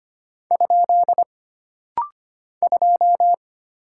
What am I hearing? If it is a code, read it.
Morse code, heard as a keyed tone, about 25 wpm, 700 Hz. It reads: ? 2
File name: Single Character (Letter:Number:Punctuation:Prosign) - Rapid-Fire - 25wpm